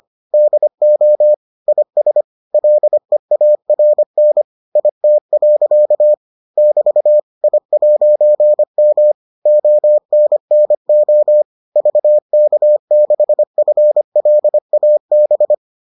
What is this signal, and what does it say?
Morse code, heard as a keyed tone, about 25 wpm, 600 Hz. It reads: DO IS LEARN IT. <BT> I'M ONNO VK6FLAB